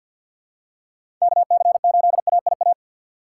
Morse code, heard as a keyed tone, about 52 wpm, 700 Hz. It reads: KC8AIA
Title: USA Call Signs - 3x Character Spacing - 50wpm